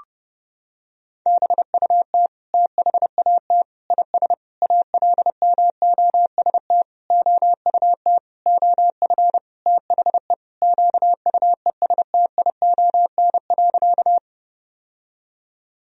Morse code, heard as a keyed tone, about 30 wpm, 700 Hz. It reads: BUT THAT IS ALMOST OUT OF THE QUESTION.
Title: Sentences from Top 1000 Words - Review Part 4 - 30wpm